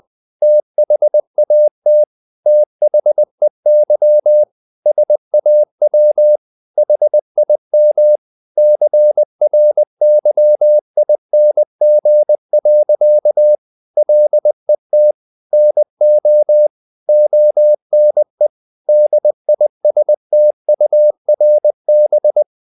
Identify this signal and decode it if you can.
Morse code, heard as a keyed tone, about 20 wpm, 600 Hz. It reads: THAT THEY SAW HIM CRYING. LET NO ONE DISTURB
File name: the old man and the sea part 2 20wpm